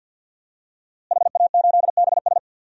Morse code, heard as a keyed tone, about 51 wpm, 700 Hz. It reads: HA8BS
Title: DXCC-100 Call Signs - 5 Characters - Repeated 2x - 50wpm